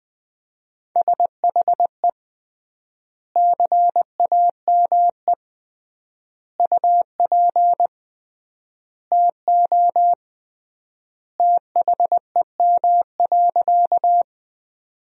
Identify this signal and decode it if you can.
Morse code, heard as a keyed tone, about 20 wpm, 700 Hz. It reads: SHE CAME UP TO THEM.
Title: 3x Word-Spacing - Sentences from Top 200 Words - 20wpm